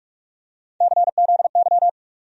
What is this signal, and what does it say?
Morse code, heard as a keyed tone, about 41 wpm, 700 Hz. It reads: CZY
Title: USA Call Sign Postfixes - 3 Letters 45wpm